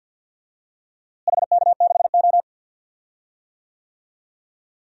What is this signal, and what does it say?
Morse code, heard as a keyed tone, about 48 wpm, 700 Hz. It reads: HK6O
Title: DXCC-100 Call Signs - 4 Characters - Repeated 2x - 50wpm